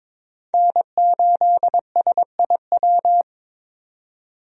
Morse code, heard as a keyed tone, about 22 wpm, 700 Hz. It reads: N8SIW
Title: USA Call Signs - 5 Characters - 22wpm